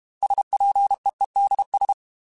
Morse code, heard as a keyed tone, about 32 wpm, 800 Hz. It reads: SPEEDS